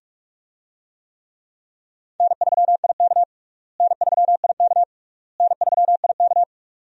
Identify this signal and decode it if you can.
Morse code, heard as a keyed tone, about 46 wpm, 700 Hz. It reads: N3IK N3IK N3IK
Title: USA Call Signs - 4 Characters - Repeated 3x - 45wpm